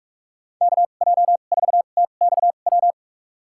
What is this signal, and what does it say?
Morse code, heard as a keyed tone, about 46 wpm, spaced as for 33 wpm, 700 Hz. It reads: KJ4TXW
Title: USA Call Signs - 6 Characters - 2x Character Spacing - 45wpm